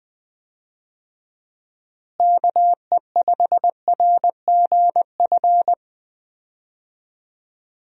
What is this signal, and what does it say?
Morse code, heard as a keyed tone, about 20 wpm, 700 Hz. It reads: KE5RGF
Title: USA Call Signs - 6 Characters - Repeated 2x - 20wpm